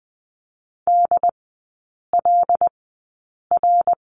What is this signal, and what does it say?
Morse code, heard as a keyed tone, about 20 wpm, spaced as for 8 wpm, 700 Hz. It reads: DLR
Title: 138 Lesson 23e - Letter f - Three Characters 20wpm